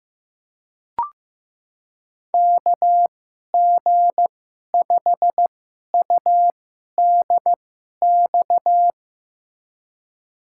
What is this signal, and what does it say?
Morse code, heard as a keyed tone, about 15 wpm, spaced as for 11 wpm, 700 Hz. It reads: KG5UDX
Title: USA Call Signs - 6 Characters - 2x Character Spacing - 15wpm